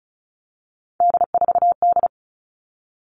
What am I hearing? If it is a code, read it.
Morse code, heard as a keyed tone, about 34 wpm, 700 Hz. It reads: D4D
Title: All Cut Numbers - 3-Digit 35wpm